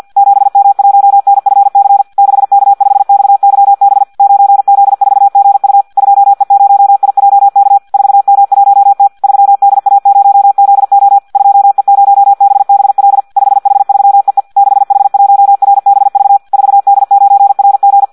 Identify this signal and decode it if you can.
Morse code, heard as a keyed tone, about 50 wpm, 800 Hz. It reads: 7M1NFC 6K5XYB 9Z4GU 2E0IJK 4M1T 3DA0ZO 2E0BBL 5H3EE 6H1RBV 4D9RG
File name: Audio practica CW / qrq-tx-50 2